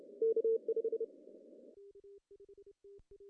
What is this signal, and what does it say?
Morse code, heard as a keyed tone, about 29 wpm, 440 Hz. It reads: K5
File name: CW / cqww-cw-k5tr